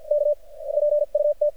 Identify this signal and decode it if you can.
Morse code, heard as a keyed tone, about 50 wpm, spaced as for 42 wpm, 590 Hz. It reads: K PUT
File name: archived / CRSnet